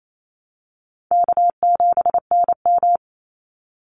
Morse code, heard as a keyed tone, about 28 wpm, 700 Hz. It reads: K7NM
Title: Easy Call Signs - 28wpm